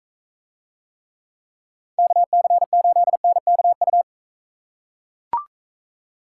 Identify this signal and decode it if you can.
Morse code, heard as a keyed tone, about 40 wpm, 700 Hz. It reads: KC8NKU